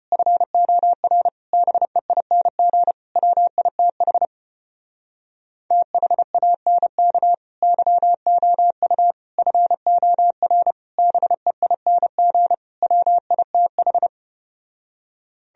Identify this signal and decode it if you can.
Morse code, heard as a keyed tone, about 33 wpm, 700 Hz. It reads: FOR BEING WITH THANK YOU FOR BEING WITH